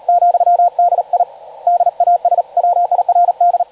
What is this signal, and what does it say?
Morse code, heard as a keyed tone, about 39 wpm, 700 Hz, with radio noise. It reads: ,D I DAS WIRD
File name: digimodes / Digimodes mp3 / CW-Hispeed